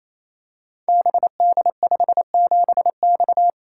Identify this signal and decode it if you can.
Morse code, heard as a keyed tone, about 27 wpm, 700 Hz. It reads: BD57X
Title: Code Groups - 5 Letter:Numbers - 28wpm